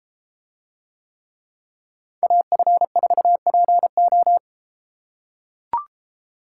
Morse code, heard as a keyed tone, about 33 wpm, 700 Hz. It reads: AF4PO